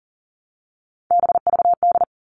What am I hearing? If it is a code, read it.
Morse code, heard as a keyed tone, about 40 wpm, 700 Hz. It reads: BVD
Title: Three Challenging Letter-Number Combinations 40wpm